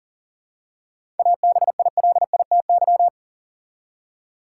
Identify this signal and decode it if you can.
Morse code, heard as a keyed tone, about 40 wpm, 700 Hz. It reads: ABILITY
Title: Seven Letter Words - 40wpm